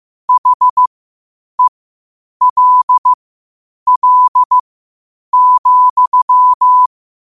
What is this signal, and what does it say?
Morse code, heard as a keyed tone, about 15 wpm, spaced as for 8 wpm, 1000 Hz. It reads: HELL,
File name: qotdfiles / 8wpm